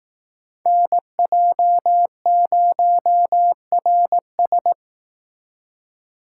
Morse code, heard as a keyed tone, about 19 wpm, 700 Hz. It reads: NJ0RS
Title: Speed Racing - USA 2x2 Call Signs 15wpm